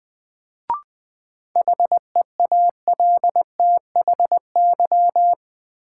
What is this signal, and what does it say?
Morse code, heard as a keyed tone, about 20 wpm, 700 Hz. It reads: HEALTHY
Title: Sentences from Top 1000 Words - Review Part 5 - New Word Drill - 20wpm